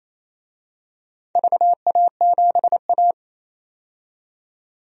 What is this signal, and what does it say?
Morse code, heard as a keyed tone, about 27 wpm, 700 Hz. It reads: VA7A
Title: DXCC-100 Call Signs - 4 Characters - 28wpm